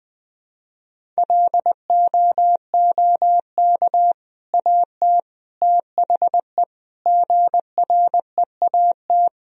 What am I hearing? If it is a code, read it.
Morse code, heard as a keyed tone, about 20 wpm, 700 Hz. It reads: LOOK AT THE GREAT
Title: Sentences from Top 1000 Words - Review Part 1 - 20wpm